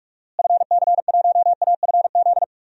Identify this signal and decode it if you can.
Morse code, heard as a keyed tone, about 42 wpm, 700 Hz. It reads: FC1AFZ
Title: DXCC-100 Call Signs - 6 Characters - Repeated 2x - 45wpm